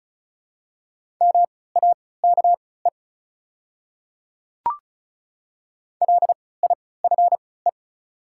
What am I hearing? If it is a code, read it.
Morse code, heard as a keyed tone, about 36 wpm, spaced as for 20 wpm, 700 Hz. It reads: MAKE LIFE
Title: 3x Character Spacing - Top 300 Words - 35wpm